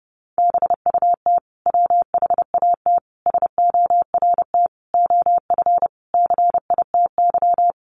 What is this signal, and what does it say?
Morse code, heard as a keyed tone, about 30 wpm, 700 Hz. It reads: BUT WHAT SORT OF CITY
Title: Sentences from Top 1300 Words 30wpm